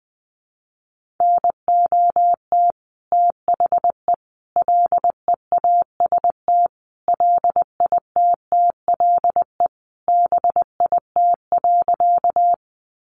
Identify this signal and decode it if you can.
Morse code, heard as a keyed tone, about 20 wpm, 700 Hz. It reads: NOT THE LEAST LITTLE BIT.
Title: Sentences from Top 900 Words 20wpm